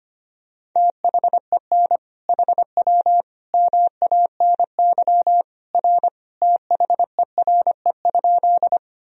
Morse code, heard as a keyed tone, about 25 wpm, 700 Hz. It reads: THEN HW MANY R THERE?
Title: Sentences from Top 100 Words and Abbreviations 25wpm